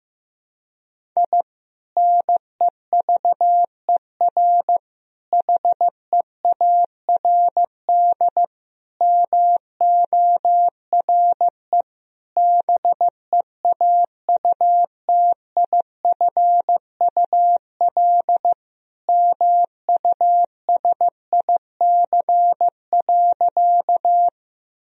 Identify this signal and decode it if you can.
Morse code, heard as a keyed tone, about 15 wpm, 700 Hz. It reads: I NEVER HEARD MORE BEAUTIFUL MUSIC.
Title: Sentences from Top 1000 Words - Review Part 1 - 15wpm